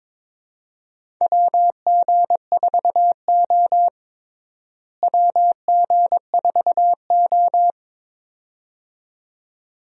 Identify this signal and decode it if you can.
Morse code, heard as a keyed tone, about 21 wpm, 700 Hz. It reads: WG4O WG4O
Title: USA Call Signs - 4 Characters - Repeated 2x - 22wpm